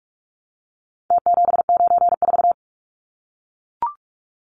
Morse code, heard as a keyed tone, about 44 wpm, 700 Hz. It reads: T794
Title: Numbers - Contest Style - 4-Digit 45wpm